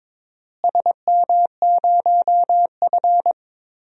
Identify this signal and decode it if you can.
Morse code, heard as a keyed tone, about 21 wpm, 700 Hz. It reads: SM0F
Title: DXCC-100 Call Signs - 4 Characters - Repeated 2x - 22wpm